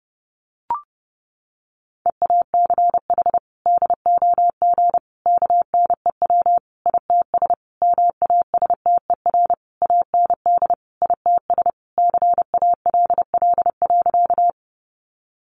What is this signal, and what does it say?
Morse code, heard as a keyed tone, about 30 wpm, 700 Hz. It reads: EACH DOG KNEW ITS MASTER AND ITS CALL.
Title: Sentences from Top 1300 Words 30wpm